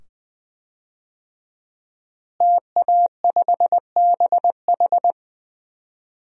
Morse code, heard as a keyed tone, about 20 wpm, 700 Hz. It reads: TA5BH
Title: DXCC-100 Call Signs - 5 Characters - Repeated 2x - 20wpm